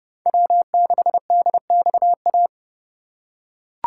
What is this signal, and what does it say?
Morse code, heard as a keyed tone, about 27 wpm, 700 Hz. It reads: W6DXA